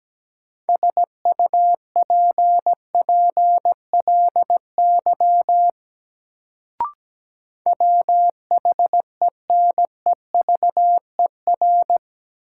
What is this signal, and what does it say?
Morse code, heard as a keyed tone, about 17 wpm, 700 Hz. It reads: SUPPLY WHENEVER
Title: Sentences from Top 1200 Words - New Word Drill Alt - 17wpm